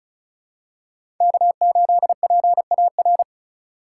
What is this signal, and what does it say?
Morse code, heard as a keyed tone, about 36 wpm, 700 Hz. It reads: K8PAR
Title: USA Call Signs - 5 Characters - 3x Character Spacing - 35wpm